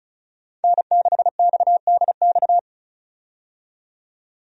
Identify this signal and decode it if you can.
Morse code, heard as a keyed tone, about 36 wpm, 700 Hz. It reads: N6XDX
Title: USA Call Signs - 5 Characters - 35wpm